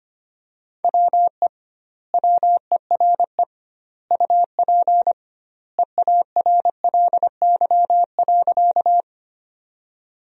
Morse code, heard as a keyed tone, about 25 wpm, 700 Hz. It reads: WE WERE UP EARLY.
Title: 2x Word-Spacing - Sentences from Top 400 Words - 25wpm